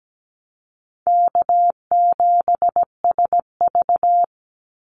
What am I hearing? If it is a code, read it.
Morse code, heard as a keyed tone, about 17 wpm, 700 Hz. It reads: K7SV
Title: Easy Call Signs Repeated 3x - 17wpm